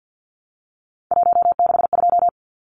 Morse code, heard as a keyed tone, about 50 wpm, 700 Hz. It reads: 162
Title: Numbers - 3-Digit 50wpm